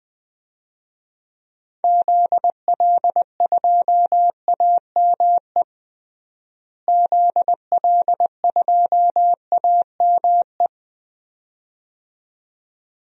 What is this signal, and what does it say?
Morse code, heard as a keyed tone, about 20 wpm, 690 Hz. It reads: ZL2AME ZL2AME
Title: DXCC-100 Call Signs - 6 Characters - Repeated 2x - 20wpm